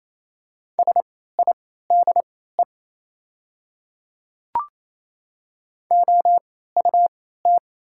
Morse code, heard as a keyed tone, about 28 wpm, spaced as for 16 wpm, 700 Hz. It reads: SIDE OUT
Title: 3x Character Spacing - Top 100 Words - 28wpm